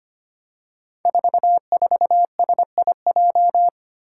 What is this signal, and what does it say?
Morse code, heard as a keyed tone, about 25 wpm, 700 Hz. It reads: 44SIJ
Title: Five Challenging Letter-Number Combinations 25wpm